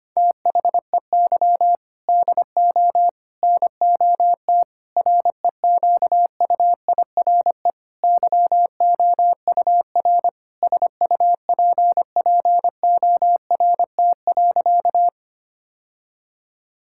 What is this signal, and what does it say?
Morse code, heard as a keyed tone, about 25 wpm, 700 Hz. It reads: THEY DO NOT REQUIRE YOUR SUPPORT.
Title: Sentences from Top 1000 Words - Review Part 6 - 25wpm